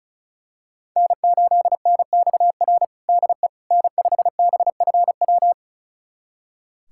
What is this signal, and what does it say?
Morse code, heard as a keyed tone, about 36 wpm, 700 Hz. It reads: N8NXR DE N5BFW
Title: USA Callsign DE Callsign - 5 Characters - 35wpm